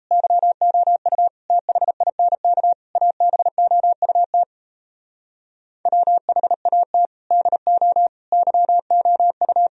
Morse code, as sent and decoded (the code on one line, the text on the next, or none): -.-- --- ..- / - .... .. -. -.- / .- -... --- ..- - / .-- .... .- - / -.. --- / -.-- --- ..-
YOU THINK ABOUT WHAT DO YOU